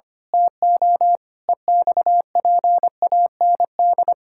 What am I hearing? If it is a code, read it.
Morse code, heard as a keyed tone, about 25 wpm, 700 Hz. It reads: TO EXPAND
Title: Pangram Challenge v2 - 25wpm